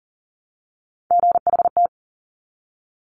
Morse code, heard as a keyed tone, about 38 wpm, 690 Hz. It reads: GHT
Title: Top Three Letter Combinations in Words 40wpm